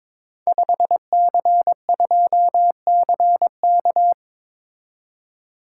Morse code, heard as a keyed tone, about 22 wpm, 690 Hz. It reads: 5C2CK
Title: Code Groups - 5 Letter:Numbers - 22wpm